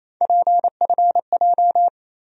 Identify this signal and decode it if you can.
Morse code, heard as a keyed tone, about 28 wpm, 700 Hz. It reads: PFJ
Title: USA Call Sign Postfixes - 3 Letters 28wpm